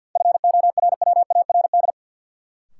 Morse code, heard as a keyed tone, about 49 wpm, 700 Hz. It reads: FORWARD
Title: Sentences from Top 1300 Words - New Word Drill - 50wpm